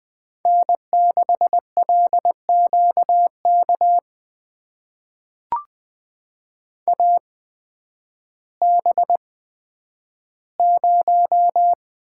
Morse code, heard as a keyed tone, about 20 wpm, spaced as for 11 wpm, 700 Hz. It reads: N6LQK AB0
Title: USA Call Signs - 5 Characters - 8x Character Spacing - 20wpm